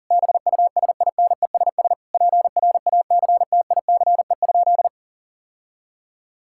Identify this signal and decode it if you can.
Morse code, heard as a keyed tone, about 37 wpm, 700 Hz. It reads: BUSINESS PRACTICE?
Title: Sentences from Top 1000 Words - Review Part 6 - 40wpm